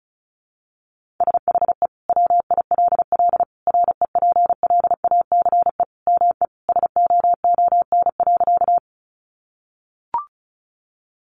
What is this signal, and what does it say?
Morse code, heard as a keyed tone, about 34 wpm, 700 Hz. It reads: SHE WILL REPLACE ME SOON.
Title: Sentences from Top 1400 Words 35wpm